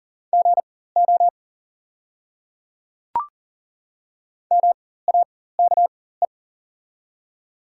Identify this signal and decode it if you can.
Morse code, heard as a keyed tone, about 40 wpm, spaced as for 19 wpm, 690 Hz. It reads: GO MAKE
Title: 4x Character Spacing - Top 200 Words - 40wpm